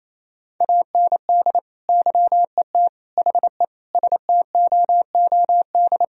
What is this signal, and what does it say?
Morse code, heard as a keyed tone, about 27 wpm, 700 Hz. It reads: AND YET HE STOOD
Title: Sentences from Top 1000 Words - Review Part 3 - 28wpm